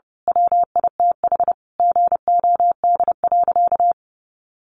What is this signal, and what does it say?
Morse code, heard as a keyed tone, about 28 wpm, 700 Hz. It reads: WITH GOD.
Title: A Sound Of Thunder 30wpm